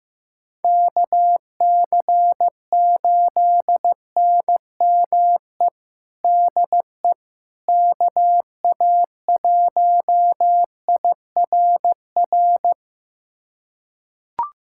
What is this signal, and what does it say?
Morse code, heard as a keyed tone, about 16 wpm, 700 Hz. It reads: KC8NME DE KA1IRR
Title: USA Callsign DE Callsign - 6 Characters - 15wpm